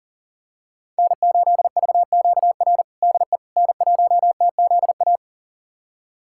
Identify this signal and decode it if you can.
Morse code, heard as a keyed tone, about 40 wpm, 700 Hz. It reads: N8VQR DE N1TZA
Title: USA Callsign DE Callsign - 5 Characters - 40wpm